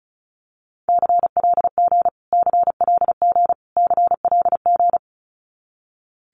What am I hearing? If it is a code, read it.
Morse code, heard as a keyed tone, about 34 wpm, 700 Hz. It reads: CLG CLG CLG
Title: Abbreviations Repeated 3x - 35wpm